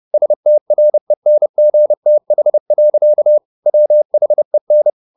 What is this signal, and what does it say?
Morse code, heard as a keyed tone, about 30 wpm, 600 Hz. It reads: STRENGTH. WHEN